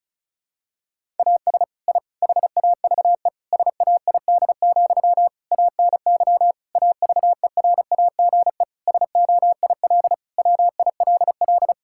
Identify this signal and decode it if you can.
Morse code, heard as a keyed tone, about 34 wpm, 700 Hz. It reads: AS I HAVE SAID, ANY AVERAGE SOIL WILL
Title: Sentences from Top 1000 Words - Review Part 6 - 35wpm